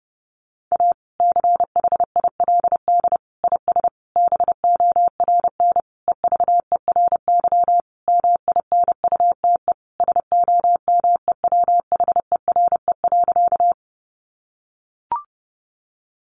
Morse code, heard as a keyed tone, about 30 wpm, 700 Hz. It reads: A CHILD IS BORN EVERY MINUTE SOMEWHERE.
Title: Sentences from Top 1400 Words 30wpm